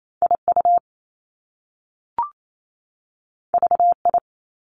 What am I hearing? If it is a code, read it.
Morse code, heard as a keyed tone, about 28 wpm, 700 Hz. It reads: I U V I
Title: Top Two Letter Combinations in Words 28wpm